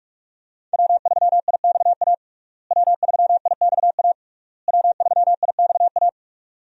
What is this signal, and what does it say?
Morse code, heard as a keyed tone, about 41 wpm, 700 Hz. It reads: W3IXA W3IXA W3IXA
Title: USA Call Signs - 5 Characters - Repeated 3x - 45wpm